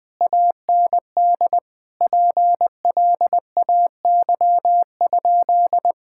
Morse code, heard as a keyed tone, about 21 wpm, 700 Hz. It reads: AND PLAY?
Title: Sentences from Top 1000 Words - Review Part 6 - 20wpm